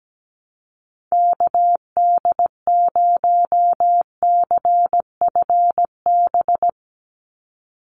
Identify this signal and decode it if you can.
Morse code, heard as a keyed tone, about 17 wpm, 700 Hz. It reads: KD0CFB
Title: USA Call Signs 17wpm